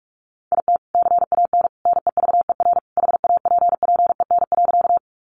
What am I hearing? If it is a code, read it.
Morse code, heard as a keyed tone, about 44 wpm, 700 Hz. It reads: IT CAN NEVER HAPPEN.
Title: Sentences from Top 400 Words 45wpm